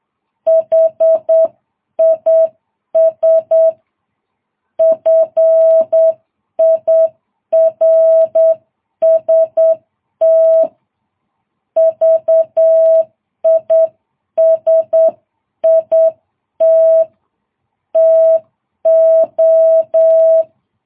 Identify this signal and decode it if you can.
Morse code, heard as a keyed tone, about 8 wpm, 650 Hz. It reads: HIS FIRST VISIT TO
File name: jh8cbh